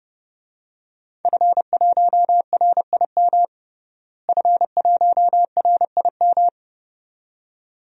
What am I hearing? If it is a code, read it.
Morse code, heard as a keyed tone, about 29 wpm, 700 Hz. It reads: F1RIM F1RIM
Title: DXCC-100 Call Signs - 5 Characters - Repeated 2x - 30wpm